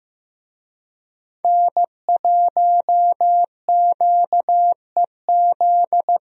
Morse code, heard as a keyed tone, about 15 wpm, 700 Hz. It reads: N1QEZ
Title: USA Call Signs - 5 Characters - Repeated 3x - 15wpm